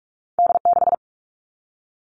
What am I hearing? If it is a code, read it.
Morse code, heard as a keyed tone, about 46 wpm, 700 Hz. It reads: D6
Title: All Cut Numbers - 2-Digit 45wpm